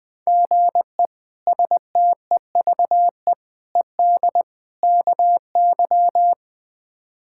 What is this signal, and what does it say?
Morse code, heard as a keyed tone, about 20 wpm, 700 Hz. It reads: GE STEVE ED KY
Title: K1USN SST QSO Practice - 20wpm